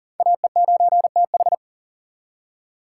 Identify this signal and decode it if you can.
Morse code, heard as a keyed tone, about 40 wpm, 700 Hz. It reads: AE9TH
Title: USA Call Signs - 4x Character Spacing - 40wpm